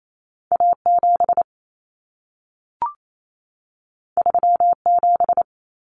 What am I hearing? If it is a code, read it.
Morse code, heard as a keyed tone, about 28 wpm, 700 Hz. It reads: A7 37
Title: Common Cut Numbers - 2-Digit 28wpm